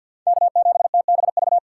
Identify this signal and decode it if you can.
Morse code, heard as a keyed tone, about 49 wpm, 700 Hz. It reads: K7TBV
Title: USA Call Signs - 5 Characters - 3x Character Spacing - 50wpm